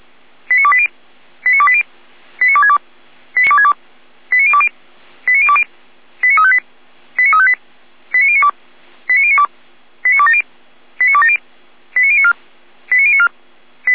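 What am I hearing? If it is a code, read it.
Five-tone selective call (ZVEI), heard as quick runs of five tones, tones of about 70 ms. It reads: 89280 89280 89262 89262 89020 89020 89358 89358 89092 89092 89270 89270 89094 89094
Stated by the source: ZVEI five-tone selective call received by SDR.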